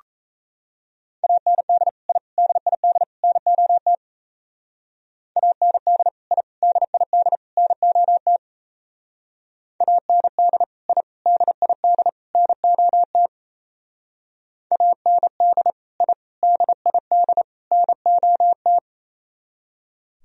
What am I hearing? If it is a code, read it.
Morse code, heard as a keyed tone, about 34 wpm, 700 Hz. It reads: AND I DID NOT AND I DID NOT AND I DID NOT AND I DID NOT
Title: Speed-Racing - Sets of 4 Words 28wpm